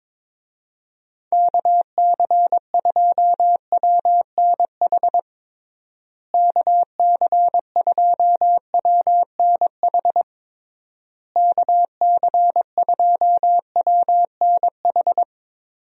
Morse code, heard as a keyed tone, about 22 wpm, 700 Hz. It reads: KC2WNH KC2WNH KC2WNH
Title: USA Call Signs - 6 Characters - Repeated 3x - 22wpm